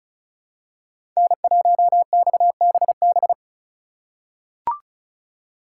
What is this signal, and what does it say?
Morse code, heard as a keyed tone, about 36 wpm, 700 Hz. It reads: N1XBB
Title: USA Call Signs - 4x Character Spacing - 35wpm